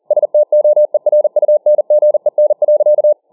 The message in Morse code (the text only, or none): STOERUNGEN.